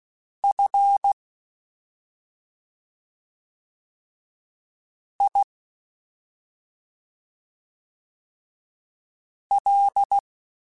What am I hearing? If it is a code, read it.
Morse code, heard as a keyed tone, about 16 wpm, spaced as for 2 wpm, 800 Hz. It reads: FIL
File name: Special INGs-mp3s / Special INGs-mp3s / Slow ing @10